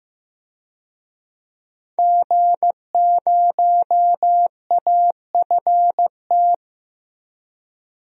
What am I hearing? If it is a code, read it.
Morse code, heard as a keyed tone, about 15 wpm, 700 Hz. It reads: G0AFT
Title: DXCC-100 Call Signs - 5 Characters - 15wpm